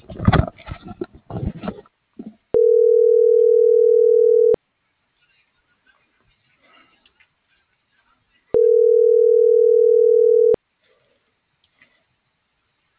Ringback tone (North America), 440 + 480 Hz.